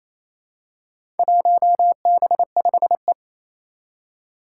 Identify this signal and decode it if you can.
Morse code, heard as a keyed tone, about 28 wpm, 700 Hz. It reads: 1B5E
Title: Four Challenging Letter-Number Combinations 28wpm